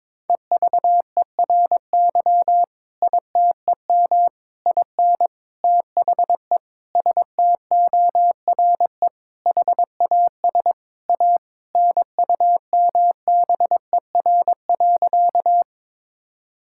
Morse code, heard as a keyed tone, about 22 wpm, 700 Hz. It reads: EVERY ITEM IN THE STORE HAS A NUMBER.
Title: Sentences from Top 1000 Words - Review Part 6 - 22wpm